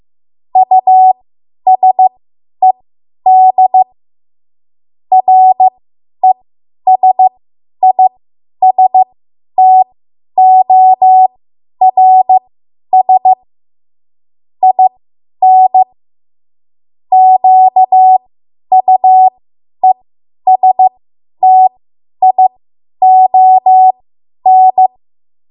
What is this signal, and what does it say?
Morse code, heard as a keyed tone, about 15 wpm, spaced as for 10 wpm, 750 Hz. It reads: USED RESISTORS IN QUESTION
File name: Morse / 10WPM